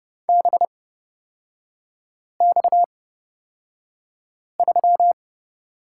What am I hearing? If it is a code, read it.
Morse code, heard as a keyed tone, about 30 wpm, spaced as for 5 wpm, 700 Hz. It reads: BX3
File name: Lesson 37e - Letter x - Three Characters 30wpm